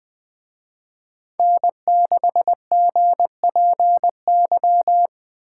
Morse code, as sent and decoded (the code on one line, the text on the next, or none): -. -.... --. .--. -.--
N6GPY